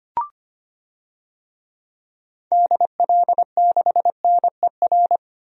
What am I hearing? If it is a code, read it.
Morse code, heard as a keyed tone, about 25 wpm, 700 Hz. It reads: DL6NER
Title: DXCC-100 Call Signs - 6 Characters - Repeated 3x - 25wpm